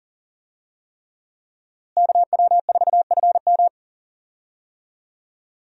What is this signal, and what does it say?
Morse code, heard as a keyed tone, about 40 wpm, 700 Hz. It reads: KW4FM
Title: USA Call Signs - 5 Characters - Repeated 3x - 40wpm